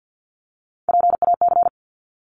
Morse code, heard as a keyed tone, about 50 wpm, 700 Hz. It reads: PAC
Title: ARRL Sections 50wpm